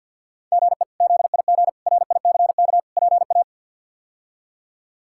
Morse code, heard as a keyed tone, about 52 wpm, 700 Hz. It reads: GE ZIG RICK PA
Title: K1USN SST QSO Practice - 50wpm